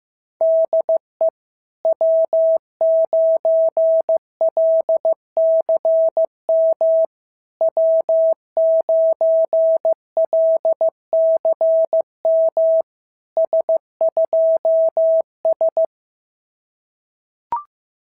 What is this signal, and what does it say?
Morse code, heard as a keyed tone, about 15 wpm, 650 Hz. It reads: DE W9LCM W9LCM S2S
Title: SOTA QSO Practice 15wpm